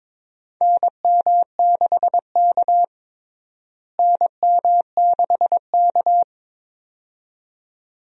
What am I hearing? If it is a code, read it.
Morse code, heard as a keyed tone, about 22 wpm, 700 Hz. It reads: NM6K NM6K
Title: USA Call Signs - 4 Characters - Repeated 2x - 22wpm